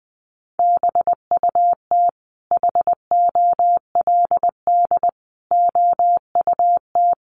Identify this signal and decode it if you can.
Morse code, heard as a keyed tone, about 19 wpm, 700 Hz. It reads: BUT HOLD OUT
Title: Sentences from Top 1300 Words 20wpm